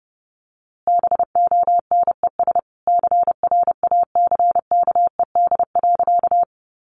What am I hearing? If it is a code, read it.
Morse code, heard as a keyed tone, about 30 wpm, 700 Hz. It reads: BONES CRACKED.
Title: A Sound Of Thunder 30wpm